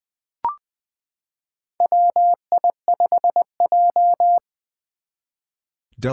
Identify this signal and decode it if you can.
Morse code, heard as a keyed tone, about 20 wpm, 700 Hz. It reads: WI5J
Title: Four Challenging Letter-Number Combinations 20wpm